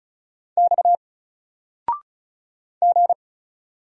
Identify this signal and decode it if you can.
Morse code, heard as a keyed tone, about 35 wpm, 700 Hz. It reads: X G
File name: Single-Letter-Number-35wpm